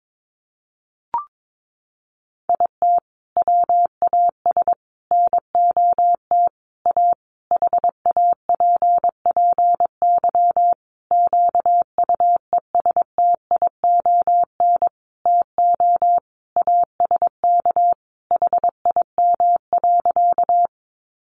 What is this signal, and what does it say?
Morse code, heard as a keyed tone, about 22 wpm, 700 Hz. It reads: IT WAS NOT A HAPPY QUESTION TO ASK HIM.
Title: Sentences from Top 1000 Words 22wpm